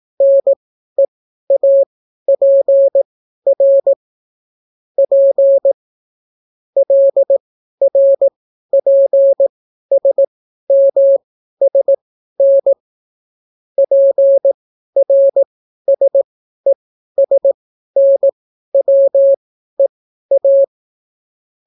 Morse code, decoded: NEAPR P LRPSMSN PRSESNWEA

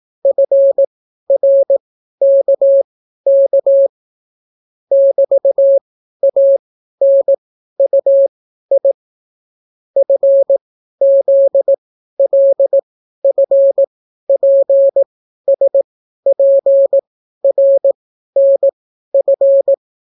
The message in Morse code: FRKK <BT>ANUI FZLFPSPRNF

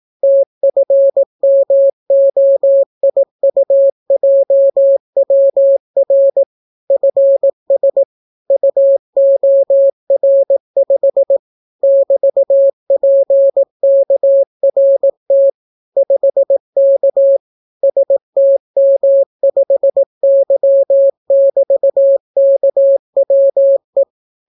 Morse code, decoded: TFMOIUJWR FS UOR5 <BT>PKRT 5K STM5Y<BT>KWE

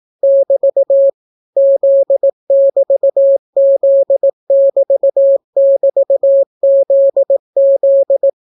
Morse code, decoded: <BT> Z<BT>Z<BT><BT>ZZ

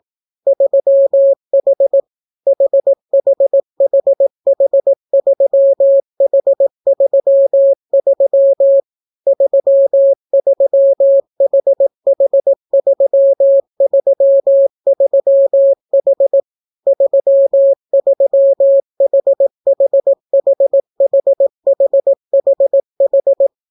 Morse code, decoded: 3H HHHH3H33 33HH333H 33HHHHHHH

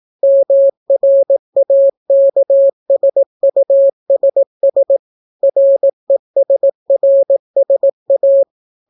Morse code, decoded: MRAKSUSS RESRSA

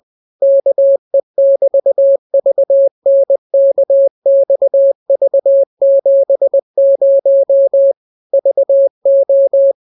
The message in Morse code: KE<BT>VNKXV70 VO

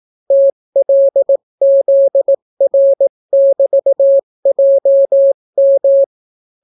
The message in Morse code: TLZR<BT>JM

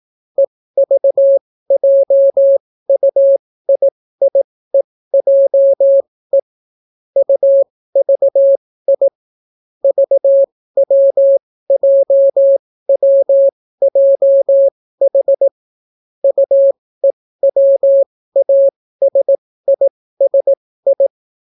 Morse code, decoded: EVJUIIEJE UVI VWJWJH UEWASISI

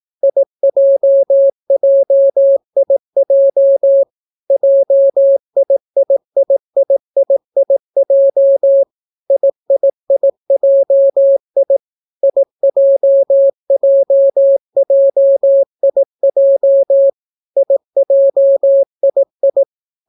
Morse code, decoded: IJJIJ JIIIIIIJ IIIJI IJJJIJ IJII

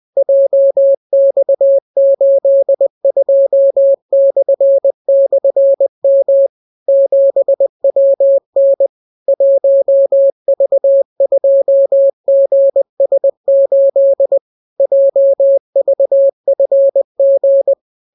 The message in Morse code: JX82//M 7WN 1V2GS8 JVFG

